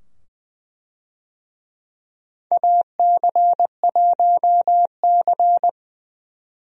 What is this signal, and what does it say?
Morse code, heard as a keyed tone, about 20 wpm, 700 Hz. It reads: AC1C
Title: USA Call Signs - 4 Characters - 20wpm